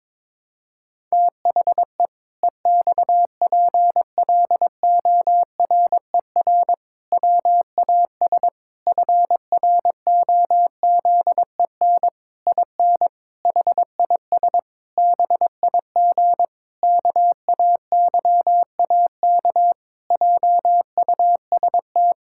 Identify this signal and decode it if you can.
Morse code, heard as a keyed tone, about 22 wpm, 700 Hz. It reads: THE EXPLORER WAS FROZEN IN HIS BIG KAYAK JUST
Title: Pangram Challenge v2 - 22wpm